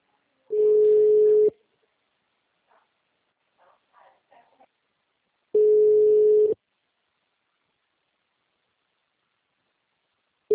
Ringback tone (Europe, 425 Hz).